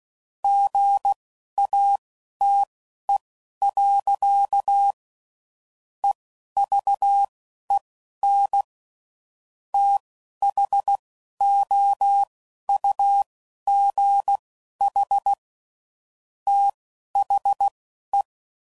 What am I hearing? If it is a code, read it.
Morse code, heard as a keyed tone, about 16 wpm, spaced as for 11 wpm, 800 Hz. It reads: GATE. EVEN THOUGH THE
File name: CW Code Groups @10WPM / Technical Subjects / AND Gate